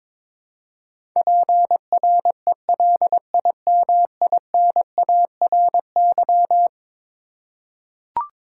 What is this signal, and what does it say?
Morse code, heard as a keyed tone, about 21 wpm, 700 Hz. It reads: PRELIMINARY
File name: practice / 22wpm / Eleven Letter Words - 22wpm